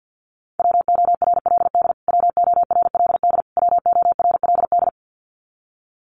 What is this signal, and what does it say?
Morse code, heard as a keyed tone, about 55 wpm, 700 Hz. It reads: WORLD WORLD WORLD
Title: Top 200 Words Repeated 50wpm